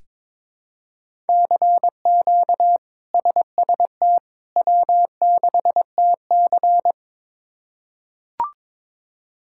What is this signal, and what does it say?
Morse code, heard as a keyed tone, about 22 wpm, 700 Hz. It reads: CQ SST W6TC
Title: K1USN SST QSO Practice - 22wpm